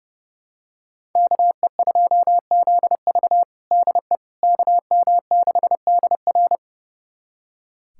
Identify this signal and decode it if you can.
Morse code, heard as a keyed tone, about 30 wpm, 700 Hz. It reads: KE2ZV DE KM6DR
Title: USA Callsign DE Callsign - 5 Characters - 30wpm